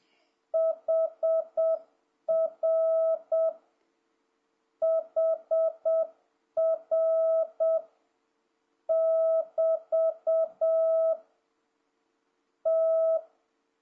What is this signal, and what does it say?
Morse code, heard as a keyed tone, about 7 wpm, 650 Hz. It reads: HR HR <BT> T
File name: jh8cbh / 2026cw